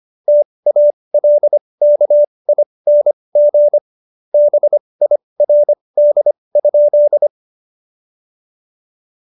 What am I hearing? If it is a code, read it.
Morse code, heard as a keyed tone, about 25 wpm, spaced as for 20 wpm, 590 Hz. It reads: TALKING BIRD?
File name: Jokes-25-char-20-eff